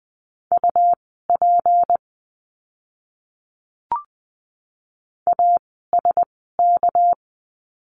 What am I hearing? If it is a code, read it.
Morse code, heard as a keyed tone, about 20 wpm, spaced as for 15 wpm, 700 Hz. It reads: UP ASK
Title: 206 Lesson 33f - Letter j - Words 20wpm